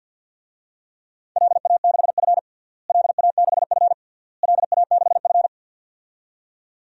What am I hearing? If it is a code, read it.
Morse code, heard as a keyed tone, about 50 wpm, 700 Hz. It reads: LA6F LA6F LA6F
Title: DXCC-100 Call Signs - 4 Characters - Repeated 3x - 50wpm